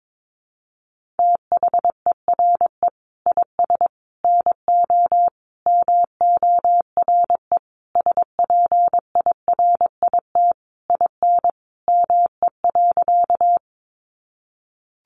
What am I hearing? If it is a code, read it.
Morse code, heard as a keyed tone, about 22 wpm, 700 Hz. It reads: THERE IS NO MORE SPIRIT IN ME.
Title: Sentences from Top 1400 Words 22wpm